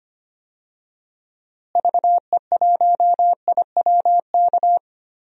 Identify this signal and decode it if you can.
Morse code, heard as a keyed tone, about 25 wpm, 700 Hz. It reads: VE1IWK
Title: DXCC-100 Call Signs - 6 Characters - Repeated 2x - 25wpm